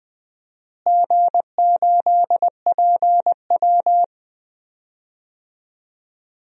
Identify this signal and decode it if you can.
Morse code, heard as a keyed tone, about 21 wpm, 700 Hz. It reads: G8PW
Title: DXCC-100 Call Signs - 4 Characters - Repeated 2x - 20wpm